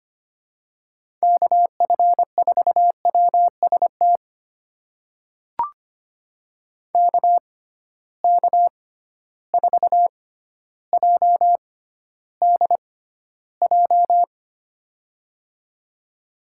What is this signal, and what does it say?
Morse code, heard as a keyed tone, about 25 wpm, spaced as for 13 wpm, 700 Hz. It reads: KF4WST KK4JDJ
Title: USA Call Signs - 6 Characters - 6x Character Spacing - 25wpm